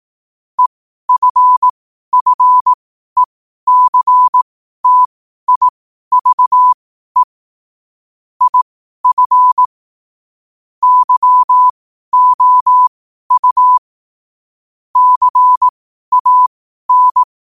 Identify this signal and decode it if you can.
Morse code, heard as a keyed tone, about 18 wpm, spaced as for 12 wpm, 1000 Hz. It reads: EFFECTIVE IF YOU CAN
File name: qotdfiles / 12wpm